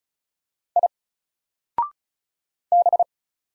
Morse code, heard as a keyed tone, about 34 wpm, 700 Hz. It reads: I B
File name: Single-Letter-Number-35wpm